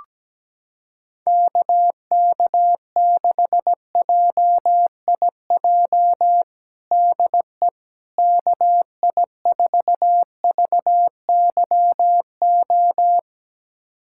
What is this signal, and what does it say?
Morse code, heard as a keyed tone, about 17 wpm, 700 Hz. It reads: KK6JIJ DE KI4VYO
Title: USA Callsign DE Callsign - 6 Characters - 17wpm